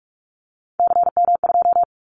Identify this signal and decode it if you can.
Morse code, heard as a keyed tone, about 44 wpm, 700 Hz. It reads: CM2
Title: Callsign Prefixes for Top 100 DXCC - 45wpm